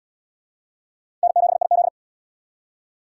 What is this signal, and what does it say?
Morse code, heard as a keyed tone, about 68 wpm, 700 Hz. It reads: N7IZ